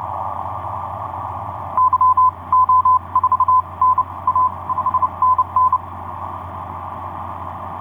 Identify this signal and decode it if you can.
Morse code, heard as a keyed tone, about 31 wpm, spaced as for 25 wpm, 1020 Hz. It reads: YO4NA 5NN
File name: z21bb g4xrx etc 10m cw 23oct2010 1201